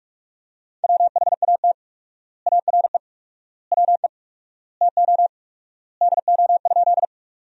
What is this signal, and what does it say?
Morse code, heard as a keyed tone, about 46 wpm, 700 Hz. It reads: WHAT ARE WE TO DO?
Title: 4x Word-Spacing - Sentences from Top 100 Words - 45wpm